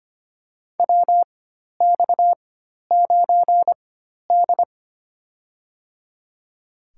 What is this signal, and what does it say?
Morse code, heard as a keyed tone, about 25 wpm, spaced as for 12 wpm, 700 Hz. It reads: WX9D
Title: USA Call Signs - 4 Characters - 4x Character Spacing - 25wpm